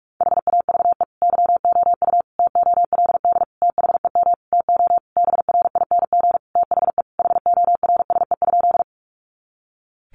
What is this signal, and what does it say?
Morse code, heard as a keyed tone, about 44 wpm, 700 Hz. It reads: HAVE YOU TOLD THEM TO BRING THE HORSE?
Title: Sentences from Top 500 Words 45wpm